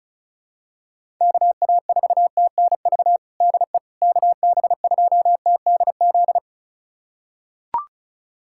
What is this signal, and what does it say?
Morse code, heard as a keyed tone, about 34 wpm, 700 Hz. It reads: KA4TNV DE KB2TDZ
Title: USA Callsign DE Callsign - 6 Characters - 35wpm